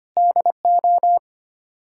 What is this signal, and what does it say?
Morse code, heard as a keyed tone, about 25 wpm, 700 Hz. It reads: DO
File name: Sets of 9 Words - 25wpm